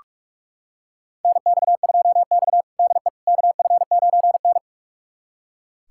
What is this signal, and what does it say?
Morse code, heard as a keyed tone, about 46 wpm, 700 Hz. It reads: NX2X DE KF9N
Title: USA Callsign DE Callsign - 4 Characters - 45wpm